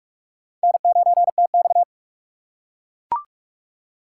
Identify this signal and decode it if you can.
Morse code, heard as a keyed tone, about 43 wpm, 700 Hz. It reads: N9TX
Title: USA Call Signs - 4 Characters - 2x Character Spacing - 45wpm